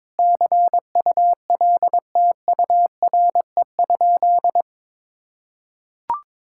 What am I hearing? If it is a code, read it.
Morse code, heard as a keyed tone, about 21 wpm, 700 Hz. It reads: CULTURE?
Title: Sentences from Top 1000 Words - Review Part 6 - 22wpm